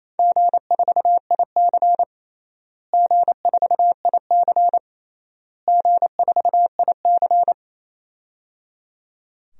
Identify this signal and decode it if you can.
Morse code, heard as a keyed tone, about 28 wpm, 700 Hz. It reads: G4IC G4IC G4IC
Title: DXCC-100 Call Signs - 4 Characters - Repeated 3x - 28wpm